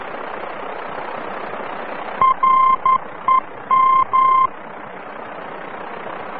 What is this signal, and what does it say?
Morse code, heard as a keyed tone, about 11 wpm, 1050 Hz, with noise. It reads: REM
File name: france / mx / REM-VOR-F